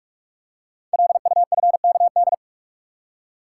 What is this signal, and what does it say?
Morse code, heard as a keyed tone, about 44 wpm, 700 Hz. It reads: LUFKD
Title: Code Groups - 5 Letters - 45wpm